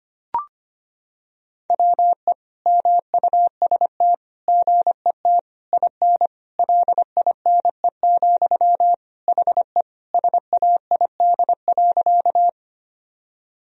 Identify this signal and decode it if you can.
Morse code, heard as a keyed tone, about 25 wpm, 700 Hz. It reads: WE MUST GET IN LINE, HE SAID.
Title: Sentences from Top 1000 Words - Review Part 1 - 25wpm